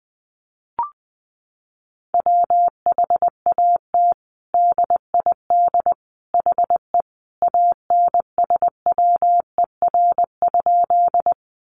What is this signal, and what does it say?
Morse code, heard as a keyed tone, about 20 wpm, 700 Hz. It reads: WHAT DID HE ANSWER?
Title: Sentences from Top 300 Words 20wpm